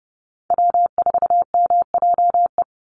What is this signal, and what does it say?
Morse code, heard as a keyed tone, about 30 wpm, 700 Hz. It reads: W4MJE
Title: Call Signs Repeated 2x - 30wpm